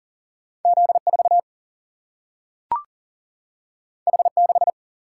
Morse code, heard as a keyed tone, about 40 wpm, 700 Hz. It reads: Z4 H6
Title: Single-Letter Single-Number 40wpm